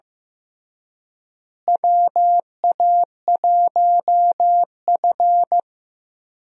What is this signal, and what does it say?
Morse code, heard as a keyed tone, about 15 wpm, 700 Hz. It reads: WA1F
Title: USA Call Signs - 4 Characters - Repeated 2x - 15wpm